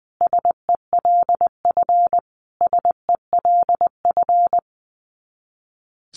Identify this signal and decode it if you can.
Morse code, heard as a keyed tone, about 20 wpm, 700 Hz. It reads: SELF SELF
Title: Top 300 Words Repeated 20wpm